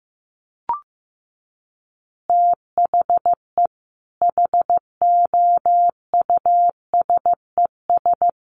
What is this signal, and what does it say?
Morse code, heard as a keyed tone, about 15 wpm, 700 Hz. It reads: THE HOUSES
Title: Easy Phrases 15wpm